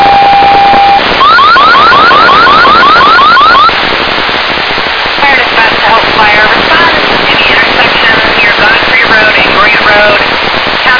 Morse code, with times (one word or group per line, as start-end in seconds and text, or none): none